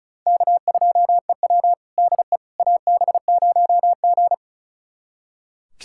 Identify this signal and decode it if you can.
Morse code, heard as a keyed tone, about 36 wpm, 690 Hz. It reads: K2EW DE AB0G
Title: USA Callsign DE Callsign - 4 Characters - 35wpm